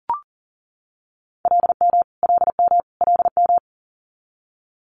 Morse code, heard as a keyed tone, about 37 wpm, 700 Hz. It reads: LM LM LM
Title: Multi-letter Phonemes Repeated 3x 40wpm